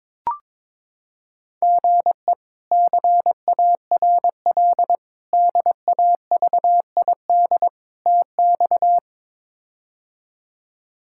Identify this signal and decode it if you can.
Morse code, heard as a keyed tone, about 22 wpm, 700 Hz. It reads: GE CARL DAVID TX
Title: K1USN SST QSO Practice - 22wpm